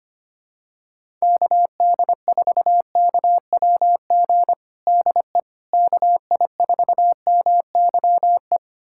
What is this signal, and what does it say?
Morse code, heard as a keyed tone, about 25 wpm, 700 Hz. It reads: KD4KWG DE KI4MYE
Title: USA Callsign DE Callsign 25wpm